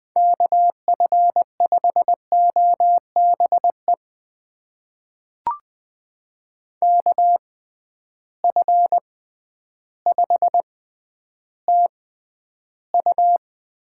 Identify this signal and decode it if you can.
Morse code, heard as a keyed tone, about 20 wpm, spaced as for 11 wpm, 700 Hz. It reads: KF5OBE KF5TU
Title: USA Call Signs - 6 Characters - 6x Character Spacing - 20wpm